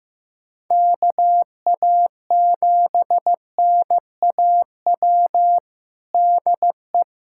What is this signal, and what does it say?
Morse code, heard as a keyed tone, about 15 wpm, 700 Hz. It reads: KA7NAW DE
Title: USA Callsign DE Callsign 15wpm